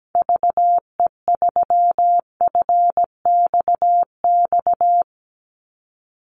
Morse code, heard as a keyed tone, about 17 wpm, 700 Hz. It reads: VE3FXX
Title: Call Signs Repeated 2x - 17wpm